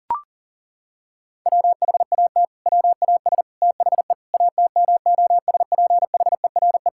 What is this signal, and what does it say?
Morse code, heard as a keyed tone, about 40 wpm, 700 Hz. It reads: WHAT WAS THE ATMOSPHERE
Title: Sentences from Top 1600 Words 40wpm